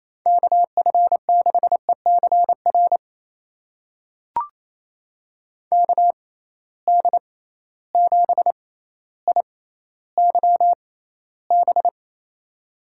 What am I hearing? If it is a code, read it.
Morse code, heard as a keyed tone, about 28 wpm, spaced as for 14 wpm, 700 Hz. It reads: KF6ECR KD7IYB
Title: USA Call Signs - 6 Characters - 6x Character Spacing - 28wpm